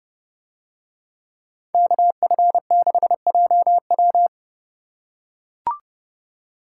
Morse code, heard as a keyed tone, about 28 wpm, 700 Hz. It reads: KF6JW